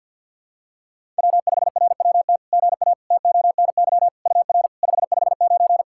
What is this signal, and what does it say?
Morse code, heard as a keyed tone, about 51 wpm, 690 Hz. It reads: W5RWT GA TONY UR 559
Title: SOTA QSO Practice 50wpm